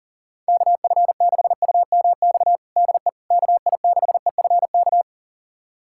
Morse code, heard as a keyed tone, about 40 wpm, 700 Hz. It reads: KF6UMX DE KI6EFK
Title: USA Callsign DE Callsign - 6 Characters - 40wpm